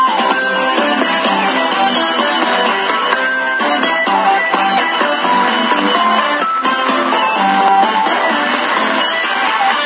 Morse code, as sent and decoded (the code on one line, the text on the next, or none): none